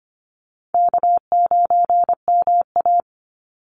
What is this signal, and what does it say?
Morse code, heard as a keyed tone, about 25 wpm, 700 Hz. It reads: K9MA
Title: Easy Call Signs Repeated 2x - 25wpm